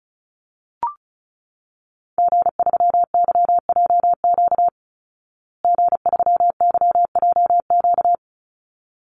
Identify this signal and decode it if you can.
Morse code, heard as a keyed tone, about 34 wpm, 700 Hz. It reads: G3YJQ G3YJQ
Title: Call Signs Repeated 3x - 35wpm